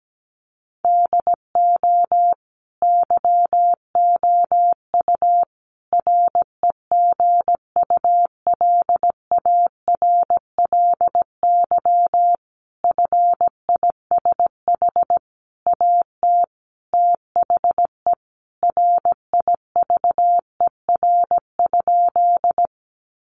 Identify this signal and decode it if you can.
Morse code, heard as a keyed tone, about 17 wpm, 700 Hz. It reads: DO YOU REGULARLY FISH AT THE RIVER?
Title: Sentences from Top 1400 Words 17wpm